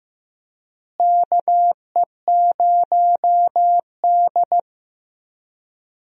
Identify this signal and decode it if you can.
Morse code, heard as a keyed tone, about 15 wpm, 700 Hz. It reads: KE0D
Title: USA Call Signs - 4 Characters - Repeated 3x - 15wpm